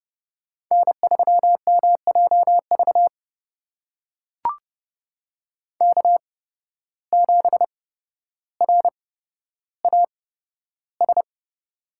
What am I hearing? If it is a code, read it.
Morse code, heard as a keyed tone, about 30 wpm, spaced as for 13 wpm, 700 Hz. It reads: N3MJV K7RAS
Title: USA Call Signs - 5 Characters - 8x Character Spacing - 30wpm